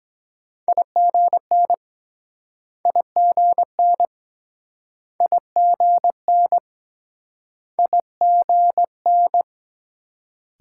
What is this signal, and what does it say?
Morse code, heard as a keyed tone, about 24 wpm, 700 Hz. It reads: IGN IGN IGN IGN